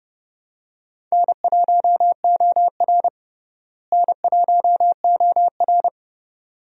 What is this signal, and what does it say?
Morse code, heard as a keyed tone, about 30 wpm, 700 Hz. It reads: N1OR N1OR
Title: USA Call Signs - 4 Characters - Repeated 2x - 30wpm